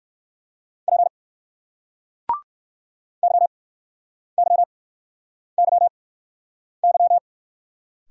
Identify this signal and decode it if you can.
Morse code, heard as a keyed tone, about 60 wpm, spaced as for 9 wpm, 700 Hz. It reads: P YYYY